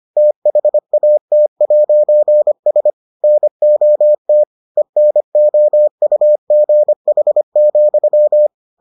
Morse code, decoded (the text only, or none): THAT'S NOT ENOUGH,